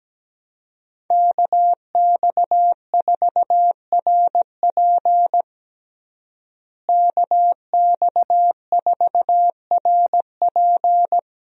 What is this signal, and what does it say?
Morse code, heard as a keyed tone, about 17 wpm, 700 Hz. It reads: KX4RP KX4RP
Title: USA Call Signs - 5 Characters - Repeated 2x - 17wpm